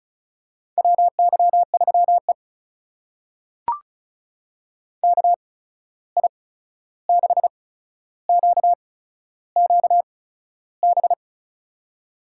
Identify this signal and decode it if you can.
Morse code, heard as a keyed tone, about 36 wpm, spaced as for 13 wpm, 700 Hz. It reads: WY3E KI6QQB
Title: USA Call Signs - 8x Character Spacing - 35wpm